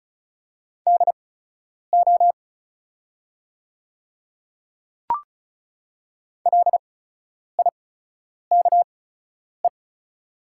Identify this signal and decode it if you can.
Morse code, heard as a keyed tone, about 36 wpm, spaced as for 10 wpm, 700 Hz. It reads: DO LIKE
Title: 8x Character Spacing - Top 100 Words - 35wpm